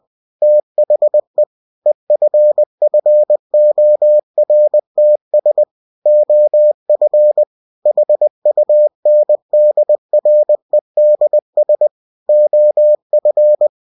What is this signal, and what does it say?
Morse code, heard as a keyed tone, about 20 wpm, 600 Hz. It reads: THE EFFORTS OF HUNDREDS OF
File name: cw-books / winnie-the-pooh / winnie-20wpm-page-0120